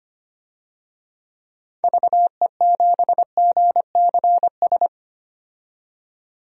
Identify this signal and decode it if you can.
Morse code, heard as a keyed tone, about 25 wpm, 700 Hz. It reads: VE7GCS
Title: DXCC-100 Call Signs - 6 Characters - 25wpm